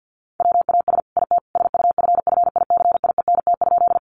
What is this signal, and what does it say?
Morse code, heard as a keyed tone, about 50 wpm, 700 Hz. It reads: WAS IT SUFFICIENT?
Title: Sentences from Top 1400 Words 50wpm